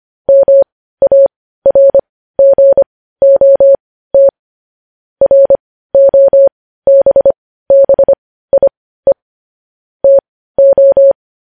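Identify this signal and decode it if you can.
Morse code, heard as a keyed tone, about 25 wpm, spaced as for 15 wpm, 550 Hz. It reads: MARGOT ROBBIE TO
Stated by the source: Morse practice of news headlines at 15 wpm.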